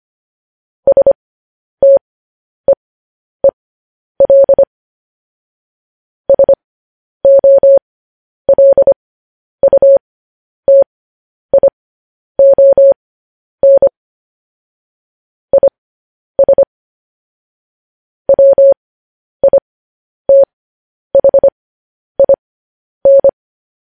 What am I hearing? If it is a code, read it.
Morse code, heard as a keyed tone, about 25 wpm, spaced as for 10 wpm, 550 Hz. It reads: STEEL SOLUTION IS WITHIN
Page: news headlines in morse code at 10 wpm